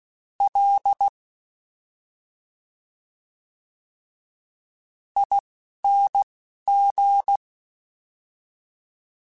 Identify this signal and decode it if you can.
Morse code, heard as a keyed tone, about 16 wpm, spaced as for 5 wpm, 800 Hz. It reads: LING
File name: Special INGs-mp3s / Special INGs-mp3s / Slow ing @10